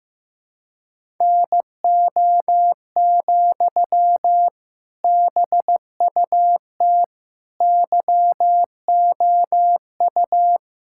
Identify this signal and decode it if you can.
Morse code, heard as a keyed tone, about 15 wpm, 700 Hz. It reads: NO, BUT YOU
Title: Sentences from Top 1000 Words - Review Part 6 - 15wpm